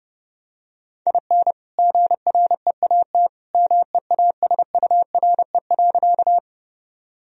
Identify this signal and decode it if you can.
Morse code, heard as a keyed tone, about 30 wpm, 700 Hz. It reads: IN GREAT MEASURE.
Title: Sentences from Top 1000 Words - Review Part 2 - 30wpm